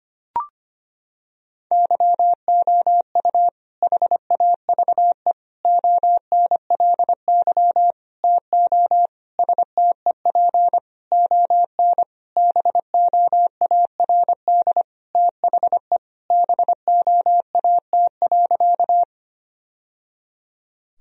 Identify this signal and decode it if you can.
Morse code, heard as a keyed tone, about 25 wpm, 700 Hz. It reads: YOU HAVE ONLY TO STEP ON BOARD THE BOAT.
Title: Sentences from Top 1000 Words - Review Part 4 - 25wpm